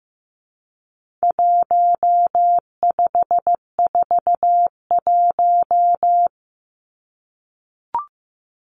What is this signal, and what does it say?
Morse code, heard as a keyed tone, about 15 wpm, 700 Hz. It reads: 1541
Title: Numbers - 4-Digit 15wpm